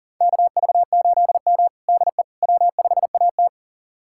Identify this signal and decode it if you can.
Morse code, heard as a keyed tone, about 40 wpm, 700 Hz. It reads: KV8M DE W5AT
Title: USA Callsign DE Callsign - 4 Characters - 40wpm